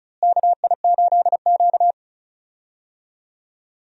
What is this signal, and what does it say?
Morse code, heard as a keyed tone, about 36 wpm, 700 Hz. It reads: KI8Q
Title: USA Call Signs - 4 Characters - Repeated 2x - 35wpm